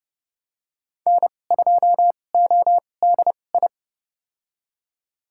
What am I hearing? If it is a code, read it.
Morse code, heard as a keyed tone, about 30 wpm, spaced as for 22 wpm, 700 Hz. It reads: N2ODI
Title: USA Call Signs - 5 Characters - 2x Character Spacing - 30wpm